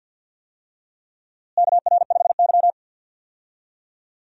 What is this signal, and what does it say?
Morse code, heard as a keyed tone, about 52 wpm, 700 Hz. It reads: KR5Y
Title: USA Call Signs - 4 Characters - Repeated 2x - 50wpm